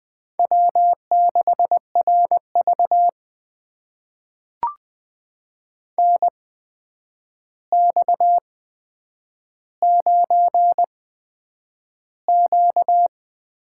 Morse code, heard as a keyed tone, about 20 wpm, spaced as for 9 wpm, 700 Hz. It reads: W6RV NX9Q
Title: USA Call Signs - 4 Characters - 8x Character Spacing - 20wpm